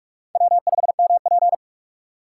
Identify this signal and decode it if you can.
Morse code, heard as a keyed tone, about 44 wpm, 700 Hz. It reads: W5MP
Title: USA Call Signs - 6x Character Spacing - 45wpm